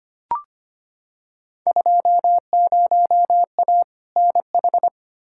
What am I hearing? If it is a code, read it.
Morse code, heard as a keyed tone, about 25 wpm, 700 Hz. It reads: 20A NH
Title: Field Day Exchanges Encore - 25wpm